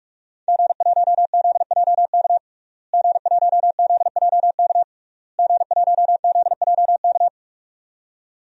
Morse code, heard as a keyed tone, about 46 wpm, 700 Hz. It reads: G1ZJK G1ZJK G1ZJK
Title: DXCC-100 Call Signs - 5 Characters - Repeated 3x - 45wpm